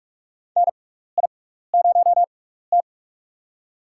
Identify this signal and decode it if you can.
Morse code, heard as a keyed tone, about 43 wpm, spaced as for 15 wpm, 700 Hz. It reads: NI0T
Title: USA Call Signs - 6x Character Spacing - 45wpm